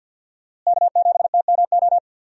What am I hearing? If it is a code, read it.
Morse code, heard as a keyed tone, about 53 wpm, 700 Hz. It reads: K7TMO
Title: USA Call Signs - 5 Characters - Repeated 2x - 50wpm